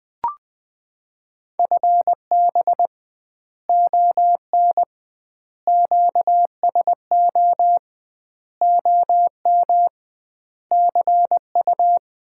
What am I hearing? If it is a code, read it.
Morse code, heard as a keyed tone, about 21 wpm, 700 Hz. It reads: FB ON QSO OM CU
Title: Common QSO Phrases - 2x Word-Spacing - 20wpm